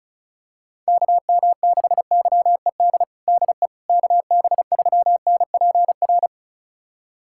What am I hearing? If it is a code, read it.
Morse code, heard as a keyed tone, about 36 wpm, 700 Hz. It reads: KM6YED DE KB3NPR
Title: USA Callsign DE Callsign - 6 Characters - 35wpm